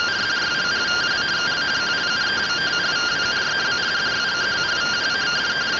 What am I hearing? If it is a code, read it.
Two-tone teleprinter signal, about 170 Hz shift.
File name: findings / MR CORONAL fr1 -131.gen-07 rtty rx